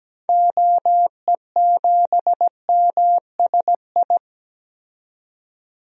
Morse code, heard as a keyed tone, about 17 wpm, 700 Hz. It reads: OE7MSI
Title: DXCC-100 Call Signs - 6 Characters - 17wpm